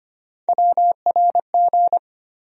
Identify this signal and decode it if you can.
Morse code, heard as a keyed tone, about 25 wpm, 700 Hz. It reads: WRG
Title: USA Call Sign Postfixes - 3 Letters 25wpm